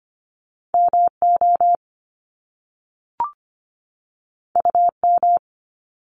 Morse code, heard as a keyed tone, about 25 wpm, 700 Hz. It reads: MO UM
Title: Top Two Letter Combinations in Words 25wpm